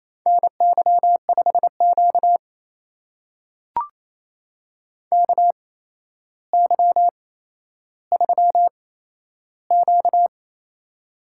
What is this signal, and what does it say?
Morse code, heard as a keyed tone, about 28 wpm, spaced as for 12 wpm, 700 Hz. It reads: NY5Q KY3Q
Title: USA Call Signs - 4 Characters - 8x Character Spacing - 28wpm